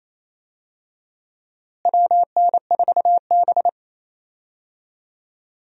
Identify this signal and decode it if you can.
Morse code, heard as a keyed tone, about 28 wpm, 700 Hz. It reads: WN4B
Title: USA Call Signs - 4 Characters - Repeated 3x - 28wpm